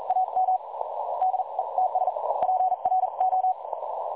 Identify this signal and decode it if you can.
Morse code, heard as a keyed tone, about 42 wpm, spaced as for 36 wpm, 750 Hz, with noise. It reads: AM DE TS PRO
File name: dlazaawansowanych / nagraniazpasma / websdr recording 3565.0kHz